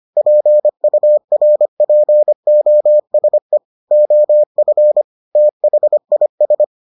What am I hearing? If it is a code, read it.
Morse code, heard as a keyed tone, about 25 wpm, 600 Hz. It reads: PURPOSE OF THIS